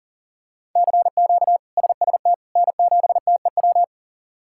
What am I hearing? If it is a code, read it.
Morse code, heard as a keyed tone, about 40 wpm, 700 Hz. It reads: CQ SST N7TEW
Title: K1USN SST QSO Practice - 40wpm